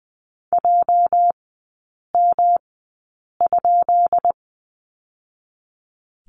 Morse code, heard as a keyed tone, about 19 wpm, 700 Hz. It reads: J M ?
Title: 212 Lesson 34e - Number 8 - Three Characters 20wpm